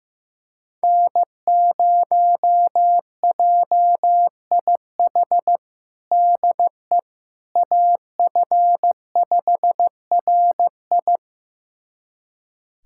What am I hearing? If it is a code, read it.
Morse code, heard as a keyed tone, about 15 wpm, 700 Hz. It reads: N0JIH DE AF5RI
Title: USA Callsign DE Callsign - 5 Characters - 15wpm